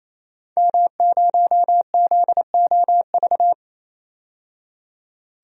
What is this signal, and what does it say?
Morse code, heard as a keyed tone, about 28 wpm, 700 Hz. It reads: M0ZOV
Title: DXCC-100 Call Signs - 5 Characters - 28wpm